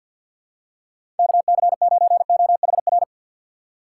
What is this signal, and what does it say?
Morse code, heard as a keyed tone, about 53 wpm, 700 Hz. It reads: KC9OHR
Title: USA Call Signs - 6 Characters - 3x Character Spacing - 50wpm